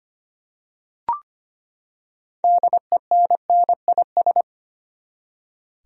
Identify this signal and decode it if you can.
Morse code, heard as a keyed tone, about 25 wpm, 700 Hz. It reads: DENNIS
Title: USA Common Names - 25wpm